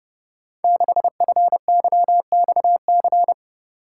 Morse code, heard as a keyed tone, about 30 wpm, 700 Hz. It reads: 6FYXC
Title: Code Groups - 5 Letter:Numbers - 30wpm